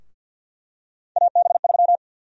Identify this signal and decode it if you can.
Morse code, heard as a keyed tone, about 49 wpm, 700 Hz. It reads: AB3
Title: USA Call Sign Prefixes - Repeated 2x - 50wpm